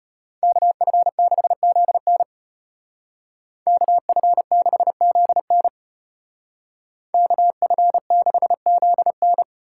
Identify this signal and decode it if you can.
Morse code, heard as a keyed tone, about 33 wpm, 700 Hz. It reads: KF6ZN KF6ZN KF6ZN